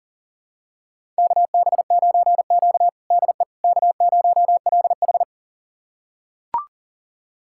Morse code, heard as a keyed tone, about 40 wpm, 700 Hz. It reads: KB9Q DE K0LH
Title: USA Callsign DE Callsign - 4 Characters - 40wpm